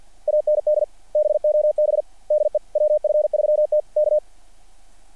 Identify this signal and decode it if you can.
Morse code, heard as a keyed tone, about 50 wpm, 600 Hz. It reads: AND BOB DE WF3T K
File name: archived / CRSnet